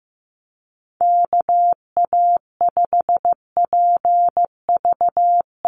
Morse code, heard as a keyed tone, about 15 wpm, 700 Hz. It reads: KA5PV
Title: NA SOTA Chaser Callsigns 15wpm